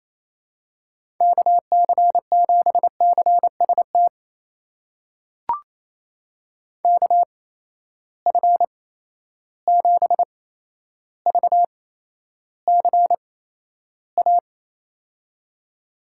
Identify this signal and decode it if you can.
Morse code, heard as a keyed tone, about 28 wpm, spaced as for 12 wpm, 700 Hz. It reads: KC7CST KF7VCA
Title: USA Call Signs - 6 Characters - 8x Character Spacing - 28wpm